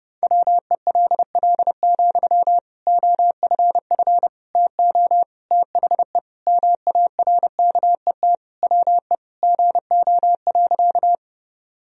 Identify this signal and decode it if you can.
Morse code, heard as a keyed tone, about 32 wpm, 700 Hz. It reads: WELL, OFF TO THE MARKET WE GO.
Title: Sentences from Top 1000 Words - Review Part 3 - 30wpm